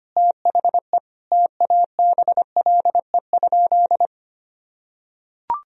Morse code, heard as a keyed tone, about 25 wpm, 700 Hz. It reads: THE TABLE?
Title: Sentences from Top 1000 Words - Review Part 4 - 25wpm